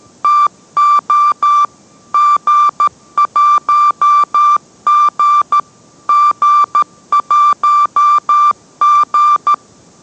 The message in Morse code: TO G1G G1G

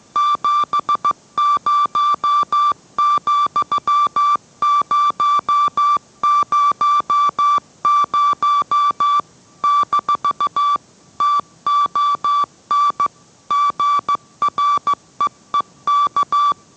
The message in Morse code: 70,000-TON GREEK

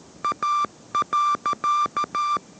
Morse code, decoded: A.

